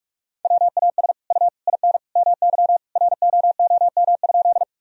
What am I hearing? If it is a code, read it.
Morse code, heard as a keyed tone, about 44 wpm, 700 Hz. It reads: WAS U IN MY ROOM?
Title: Sentences from Top 300 Words and Abbreviations 45wpm